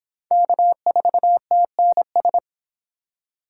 Morse code, heard as a keyed tone, about 25 wpm, 700 Hz. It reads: K4TNS